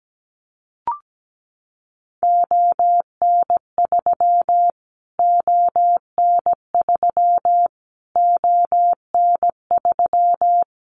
Morse code, heard as a keyed tone, about 17 wpm, 700 Hz. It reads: ON3 ON3 ON3
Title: Callsign Prefixes for Top 25 DXCC - Repeated 3x - 17wpm